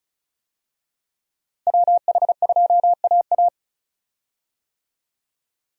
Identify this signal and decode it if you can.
Morse code, heard as a keyed tone, about 36 wpm, 700 Hz. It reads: WH2AA
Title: USA Call Signs - 5 Characters - 35wpm